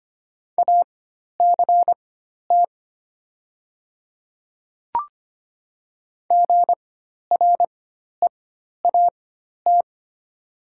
Morse code, heard as a keyed tone, about 25 wpm, spaced as for 12 wpm, 700 Hz. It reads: ACT GREAT
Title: 4x Character Spacing - Top 500 Words - 25wpm